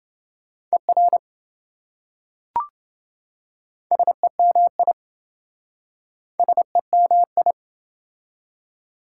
Morse code, heard as a keyed tone, about 27 wpm, 700 Hz. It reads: ER SEMI SEMI